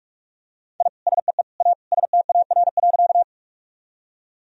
Morse code, heard as a keyed tone, about 43 wpm, 700 Hz. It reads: I SEE A STAR.
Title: Sentences from Top 1000 Words - Review Part 2 - 45wpm